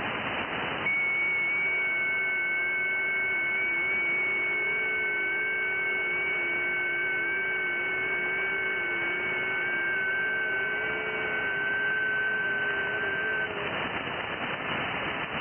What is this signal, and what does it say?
FT8.
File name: IC-7100